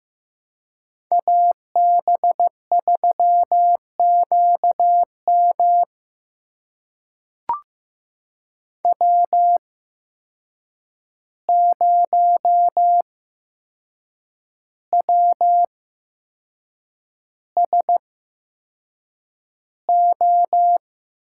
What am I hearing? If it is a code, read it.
Morse code, heard as a keyed tone, about 15 wpm, spaced as for 6 wpm, 700 Hz. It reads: AB3QM W0WSO
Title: USA Call Signs - 5 Characters - 8x Character Spacing - 15wpm